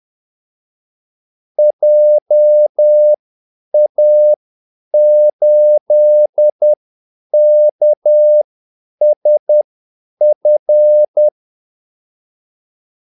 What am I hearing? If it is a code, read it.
Morse code, heard as a keyed tone, about 10 wpm, spaced as for 8 wpm, 600 Hz. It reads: JA8KSF
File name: Callsigns-10-8wpm